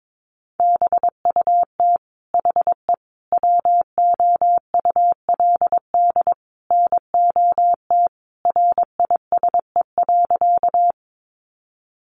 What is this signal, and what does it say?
Morse code, heard as a keyed tone, about 22 wpm, 700 Hz. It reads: BUT HE WOULD NOT RISE.
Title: Sentences from Top 900 Words 22wpm